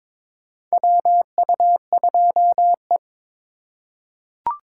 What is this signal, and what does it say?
Morse code, heard as a keyed tone, about 22 wpm, 700 Hz. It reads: WU2E
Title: USA Call Signs - 4 Characters - 2x Character Spacing - 22wpm